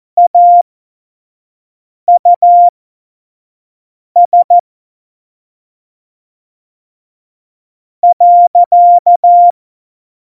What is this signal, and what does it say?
Morse code, heard as a keyed tone, about 14 wpm, spaced as for 4 wpm, 700 Hz. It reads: AUS.
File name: Morsekurs / Witze / Witz 22. Oktober 2019, 25 F70